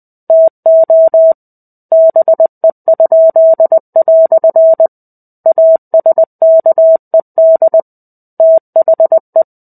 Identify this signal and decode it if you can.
Morse code, heard as a keyed tone, about 20 wpm, 650 Hz. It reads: TO BE?" ASKED THE
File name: 05-Chapter-05 20 WPM